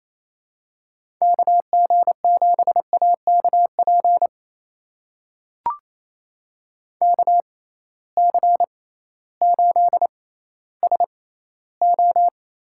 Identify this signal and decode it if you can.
Morse code, heard as a keyed tone, about 28 wpm, spaced as for 15 wpm, 700 Hz. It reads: KG7AKP KC8SO
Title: USA Call Signs - 6 Characters - 6x Character Spacing - 28wpm